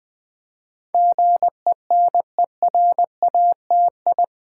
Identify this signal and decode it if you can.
Morse code, heard as a keyed tone, about 20 wpm, 700 Hz. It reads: GENERATI
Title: Build A Word - 4 Letters At A Time - 10-Letter Words - 20wpm